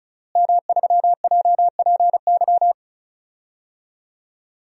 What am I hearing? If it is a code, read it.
Morse code, heard as a keyed tone, about 36 wpm, 700 Hz. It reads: M3JPY
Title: DXCC-100 Call Signs - 5 Characters - Repeated 3x - 35wpm